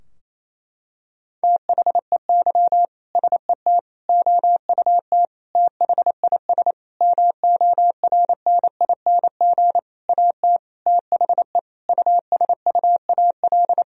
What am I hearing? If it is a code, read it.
Morse code, heard as a keyed tone, about 28 wpm, 700 Hz. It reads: THEY SET OUT THIS MORNING AT THE USUAL
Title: Sentences from Top 1000 Words - Review Part 2 - 28wpm